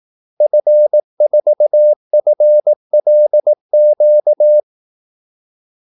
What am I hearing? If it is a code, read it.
Morse code, heard as a keyed tone, about 18 wpm, 600 Hz. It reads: F4FLQ
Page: Kaldesignaler 18 wpm